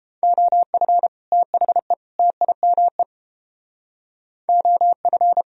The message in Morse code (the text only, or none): OF THE TIME OF